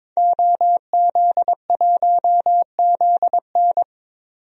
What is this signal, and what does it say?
Morse code, heard as a keyed tone, about 23 wpm, 700 Hz. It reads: OZ1ZN
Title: DXCC-100 Call Signs - 5 Characters - 22wpm